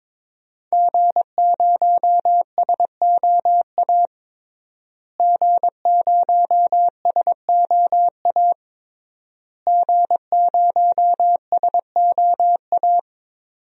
Morse code, heard as a keyed tone, about 22 wpm, 700 Hz. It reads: G0SOA G0SOA G0SOA
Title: DXCC-100 Call Signs - 5 Characters - Repeated 3x - 22wpm